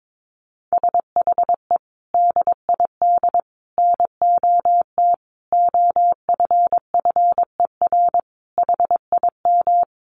Morse code, heard as a keyed tone, about 22 wpm, 700 Hz. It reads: SHE DID NOT OFFER HIM
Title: Sentences from Top 1500 Words 22wpm